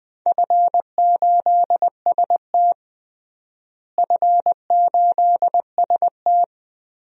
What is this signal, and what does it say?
Morse code, heard as a keyed tone, about 19 wpm, 700 Hz. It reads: F8ST F8ST
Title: DXCC-100 Call Signs - 4 Characters - Repeated 3x - 20wpm